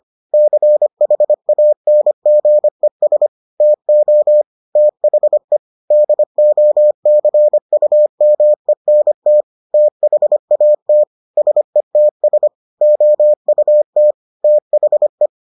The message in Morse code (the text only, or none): CHANGES TO THE DOCUMENT THAT SETS OUT THE